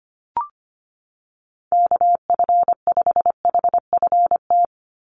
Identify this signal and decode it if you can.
Morse code, heard as a keyed tone, about 25 wpm, 700 Hz. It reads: KF5HFT
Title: USA Call Signs 25wpm